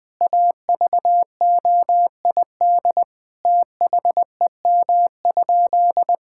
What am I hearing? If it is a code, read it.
Morse code, heard as a keyed tone, about 21 wpm, 700 Hz. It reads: AVOID THEM?
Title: Sentences from Top 1000 Words - Review Part 6 - 20wpm